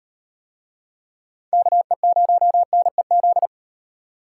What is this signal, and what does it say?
Morse code, heard as a keyed tone, about 35 wpm, 700 Hz. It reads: KE0NEZ